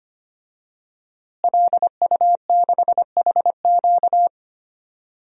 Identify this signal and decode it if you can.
Morse code, heard as a keyed tone, about 25 wpm, 700 Hz. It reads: LU6HQ
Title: DXCC-100 Call Signs - 5 Characters - Repeated 2x - 25wpm